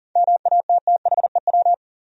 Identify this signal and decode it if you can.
Morse code, heard as a keyed tone, about 40 wpm, 700 Hz. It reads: MATTHEW
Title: USA Common Names - 40wpm